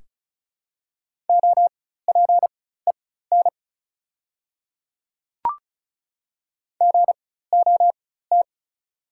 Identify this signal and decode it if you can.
Morse code, heard as a keyed tone, about 36 wpm, spaced as for 16 wpm, 690 Hz. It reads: OPEN GOT
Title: 4x Character Spacing - Top 300 Words - 35wpm